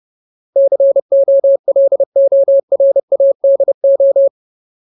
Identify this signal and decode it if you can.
Morse code, heard as a keyed tone, about 30 wpm, 550 Hz. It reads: COLORADO
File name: Ultimate Word List / 30WPM Ultimate